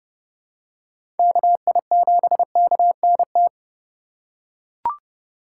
Morse code, heard as a keyed tone, about 30 wpm, 700 Hz. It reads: KI7KNT
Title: USA Call Signs - 6 Characters - 4x Character Spacing - 30wpm